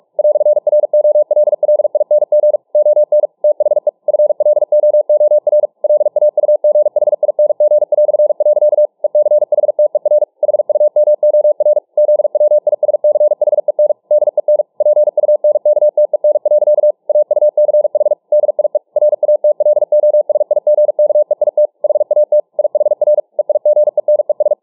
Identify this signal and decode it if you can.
Morse code, heard as a keyed tone, about 43 wpm, 600 Hz. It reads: "ROLLING ON THE FLOOR LAUGHING". ECHTER HUMOR ZWISCHEN DEN PUNKTEN. AUCH DIE RATLOSIGKEIT HAT IHR EIGENES